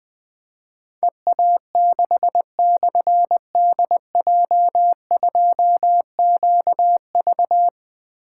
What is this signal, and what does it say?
Morse code, heard as a keyed tone, about 20 wpm, 700 Hz. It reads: EA6/DJ2QV
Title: Difficult Call Signs 20wpm